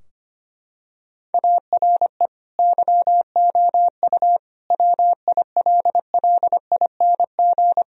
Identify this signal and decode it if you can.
Morse code, heard as a keyed tone, about 25 wpm, 700 Hz. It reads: ARE YOU WILLING
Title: Sentences from Top 1000 Words - Review Part 4 - 25wpm